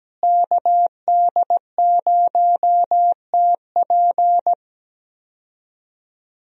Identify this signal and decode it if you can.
Morse code, heard as a keyed tone, about 17 wpm, 700 Hz. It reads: KD0TP
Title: USA Call Signs - 5 Characters - Repeated 3x - 17wpm